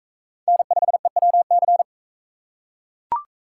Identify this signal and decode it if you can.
Morse code, heard as a keyed tone, about 41 wpm, 700 Hz. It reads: N5EWC